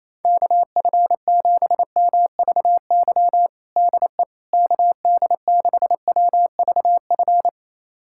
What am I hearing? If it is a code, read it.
Morse code, heard as a keyed tone, about 27 wpm, 700 Hz. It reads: KF7MVY DE KD6WVF
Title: USA Callsign DE Callsign - 6 Characters - 28wpm